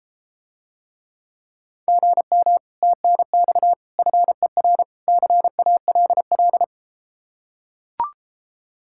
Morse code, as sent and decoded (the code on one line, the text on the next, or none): --. -- / - -. -..- / ..-. .-. / -.-. .- .-.. .-..
GM TNX FER CALL